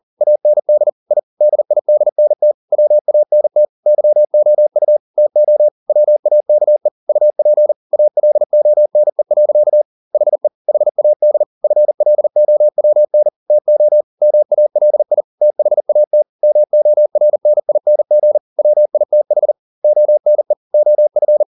Morse code, heard as a keyed tone, about 40 wpm, 600 Hz. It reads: AND I DIDNT WANT YOU TO WAKE UP ALONE. HE HAD FLOWN TO MALI THAT MORNING WITH ONE OF